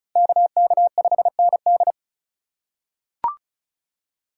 Morse code, heard as a keyed tone, about 36 wpm, 700 Hz. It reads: KK5ND
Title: USA Call Signs - 3x Character Spacing - 35wpm